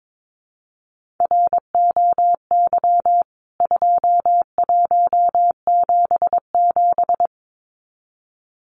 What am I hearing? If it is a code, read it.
Morse code, heard as a keyed tone, about 22 wpm, 700 Hz. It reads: ROY 2177
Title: CWT Exchange 22wpm